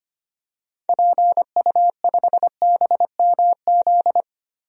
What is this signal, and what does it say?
Morse code, heard as a keyed tone, about 25 wpm, 700 Hz. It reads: PU5BMZ
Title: DXCC-100 Call Signs - 6 Characters - Repeated 3x - 25wpm